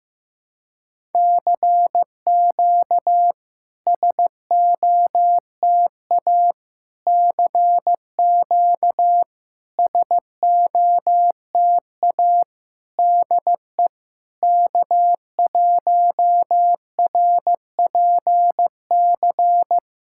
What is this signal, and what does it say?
Morse code, heard as a keyed tone, about 15 wpm, 700 Hz. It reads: CQ SOTA CQ SOTA DE K1RPC
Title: SOTA QSO Practice 15wpm